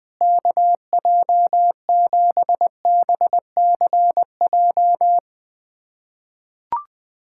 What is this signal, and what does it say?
Morse code, heard as a keyed tone, about 21 wpm, 700 Hz. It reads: KJ7BCJ
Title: USA Callsign DE Callsign 20wpm